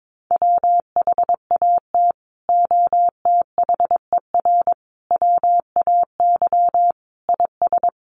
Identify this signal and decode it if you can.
Morse code, heard as a keyed tone, about 22 wpm, 700 Hz. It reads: WHAT OTHER WAY IS
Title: Sentences from Top 100 Words 22wpm